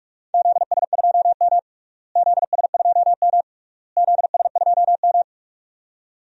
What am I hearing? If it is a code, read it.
Morse code, heard as a keyed tone, about 46 wpm, 700 Hz. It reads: ZS2M ZS2M ZS2M
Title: DXCC-100 Call Signs - 4 Characters - Repeated 3x - 45wpm